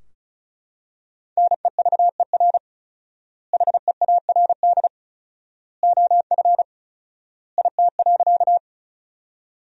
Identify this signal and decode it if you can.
Morse code, heard as a keyed tone, about 36 wpm, 700 Hz. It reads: NEVER HEARD OF IT.
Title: 4x Word-Spacing - Sentences from Top 400 Words - 35wpm